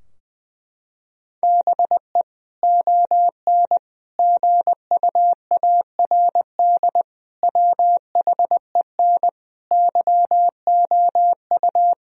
Morse code, heard as a keyed tone, about 20 wpm, 700 Hz. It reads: BE ON GUARD WHEN YOU
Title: Sentences from Top 1000 Words - Review Part 4 - 20wpm